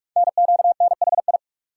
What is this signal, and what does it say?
Morse code, heard as a keyed tone, about 44 wpm, 700 Hz. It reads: NQNHI
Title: Code Groups - 5 Letters - 45wpm